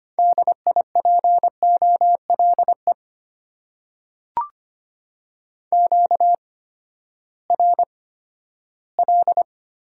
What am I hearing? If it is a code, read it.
Morse code, heard as a keyed tone, about 25 wpm, spaced as for 14 wpm, 700 Hz. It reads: DIPOLE QRL
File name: Top 90 QSO Elements - 8x Character Spacing - 25wpm